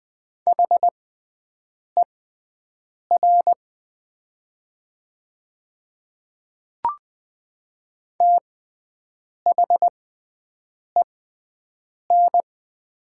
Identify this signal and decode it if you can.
Morse code, heard as a keyed tone, about 20 wpm, spaced as for 7 wpm, 700 Hz. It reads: HER THEN
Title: 6x Character Spacing - Top 200 Words - 20wpm